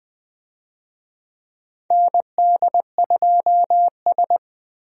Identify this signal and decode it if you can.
Morse code, heard as a keyed tone, about 21 wpm, 700 Hz. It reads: ND2S
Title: USA Call Signs - 4 Characters - Repeated 3x - 20wpm